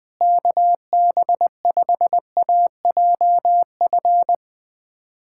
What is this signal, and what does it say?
Morse code, heard as a keyed tone, about 20 wpm, 700 Hz. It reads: KB5AJF
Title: USA Callsign DE Callsign 20wpm